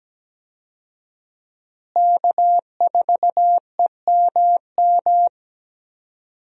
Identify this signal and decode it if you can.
Morse code, heard as a keyed tone, about 17 wpm, 700 Hz. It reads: K4EMM
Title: USA Call Signs - 5 Characters - Repeated 2x - 17wpm